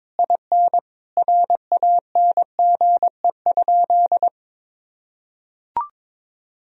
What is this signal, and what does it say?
Morse code, heard as a keyed tone, about 22 wpm, 700 Hz. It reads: IN RANGE?
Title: Sentences from Top 1000 Words - Review Part 4 - 22wpm